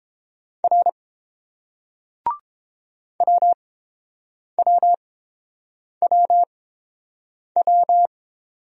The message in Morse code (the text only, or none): R WWWW